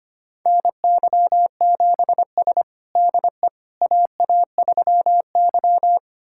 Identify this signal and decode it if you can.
Morse code, heard as a keyed tone, about 25 wpm, 700 Hz. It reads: NY7S DE AA3Y
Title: USA Callsign DE Callsign - 4 Characters - 25wpm